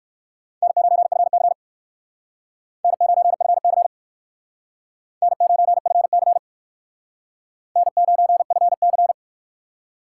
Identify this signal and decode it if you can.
Morse code, heard as a keyed tone, about 56 wpm, 700 Hz. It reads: N9FC N9FC N9FC N9FC
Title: Speed Racing - USA 1x2 Call Signs 45wpm